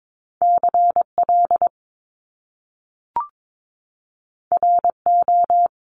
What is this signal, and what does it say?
Morse code, heard as a keyed tone, about 22 wpm, 700 Hz. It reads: CL RO
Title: Top Two Letter Combinations in Words 22wpm